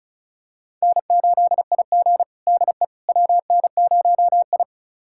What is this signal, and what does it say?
Morse code, heard as a keyed tone, about 34 wpm, 700 Hz. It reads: N8IG DE WN0I
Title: USA Callsign DE Callsign - 4 Characters - 35wpm